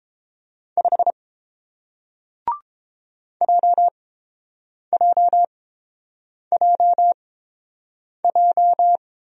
Morse code, heard as a keyed tone, about 30 wpm, spaced as for 7 wpm, 700 Hz. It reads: 5 JJJJ